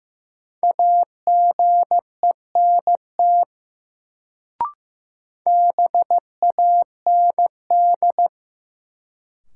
Morse code, heard as a keyed tone, about 15 wpm, 700 Hz. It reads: AGENT BAND
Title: Sentences from Top 1600 Words - New Word Drill Alt - 15wpm